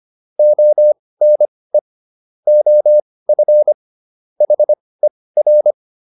Morse code, heard as a keyed tone, about 25 wpm, spaced as for 18 wpm, 600 Hz. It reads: ONE OF HER